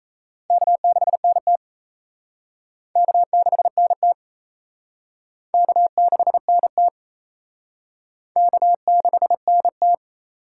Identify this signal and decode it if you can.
Morse code, heard as a keyed tone, about 35 wpm, 700 Hz. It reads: K6NT K6NT K6NT K6NT
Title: Speed Racing - USA 1x2 Call Signs 28wpm